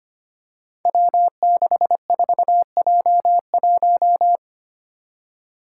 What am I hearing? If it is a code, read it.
Morse code, heard as a keyed tone, about 25 wpm, 700 Hz. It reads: W64J1
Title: Five Challenging Letter-Number Combinations 25wpm